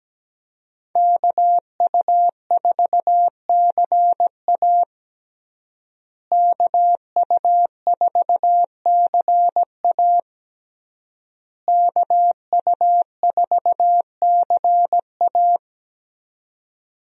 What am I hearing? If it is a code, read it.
Morse code, heard as a keyed tone, about 17 wpm, 700 Hz. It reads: KU4CA KU4CA KU4CA
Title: USA Call Signs - 5 Characters - Repeated 3x - 17wpm